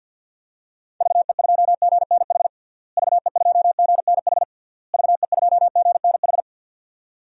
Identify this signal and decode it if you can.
Morse code, heard as a keyed tone, about 51 wpm, 700 Hz. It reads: VE2GNH VE2GNH VE2GNH
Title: DXCC-100 Call Signs - 6 Characters - Repeated 3x - 50wpm